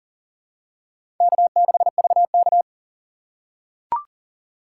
Morse code, heard as a keyed tone, about 40 wpm, 700 Hz. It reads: K6VK
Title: USA Call Signs - 3x Character Spacing - 40wpm